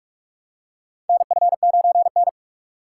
Morse code, heard as a keyed tone, about 44 wpm, 700 Hz. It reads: NF9N
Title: USA Call Signs - 4 Characters - 2x Character Spacing - 45wpm